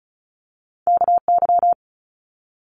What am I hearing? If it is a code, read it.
Morse code, heard as a keyed tone, about 35 wpm, 700 Hz. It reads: KY
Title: Multi-letter Phonemes Quick Fire 35wpm